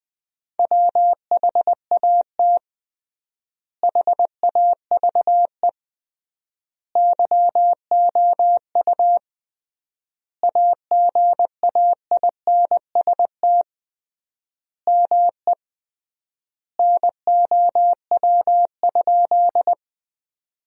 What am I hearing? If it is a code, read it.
Morse code, heard as a keyed tone, about 21 wpm, 700 Hz. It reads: WHAT HAVE YOU AGAINST ME NOW?
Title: 3x Word-Spacing - Sentences from Top 400 Words - 20wpm